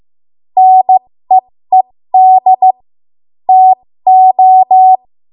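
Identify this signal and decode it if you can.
Morse code, heard as a keyed tone, about 15 wpm, spaced as for 13 wpm, 750 Hz. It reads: NEED TO
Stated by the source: 13 wpm Morse code practice archive transmission.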